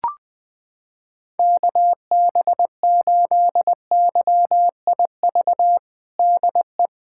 Morse code, heard as a keyed tone, about 20 wpm, 700 Hz. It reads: KB8YIV DE
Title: USA Callsign DE Callsign - 6 Characters - 20wpm